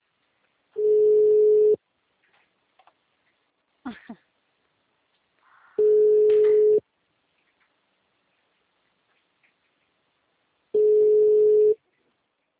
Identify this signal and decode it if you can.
Ringback tone (Europe, 425 Hz).